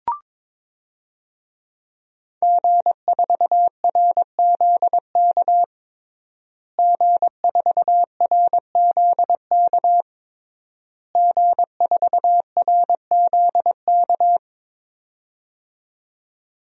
Morse code, heard as a keyed tone, about 22 wpm, 700 Hz. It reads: G4RZK G4RZK G4RZK
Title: DXCC-100 Call Signs - 5 Characters - Repeated 3x - 22wpm